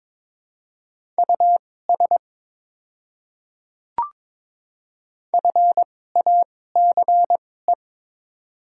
Morse code, heard as a keyed tone, about 22 wpm, spaced as for 16 wpm, 700 Hz. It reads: US FACE
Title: 2x Character Spacing - Top 300 Words - 22wpm